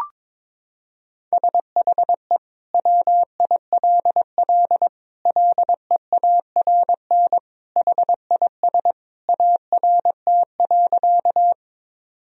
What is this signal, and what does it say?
Morse code, heard as a keyed tone, about 22 wpm, 700 Hz. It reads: SHE WILL LEARN HIS ART.
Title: Sentences from Top 1000 Words - Review Part 3 - 22wpm